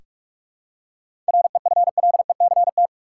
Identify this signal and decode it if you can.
Morse code, heard as a keyed tone, about 46 wpm, 700 Hz. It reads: REFLECT
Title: Seven Letter Words - 45wpm